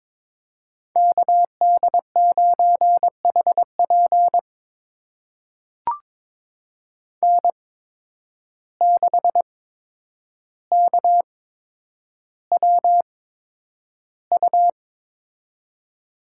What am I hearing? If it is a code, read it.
Morse code, heard as a keyed tone, about 22 wpm, spaced as for 9 wpm, 700 Hz. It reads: KD9HP N6KWU
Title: USA Call Signs - 5 Characters - 8x Character Spacing - 22wpm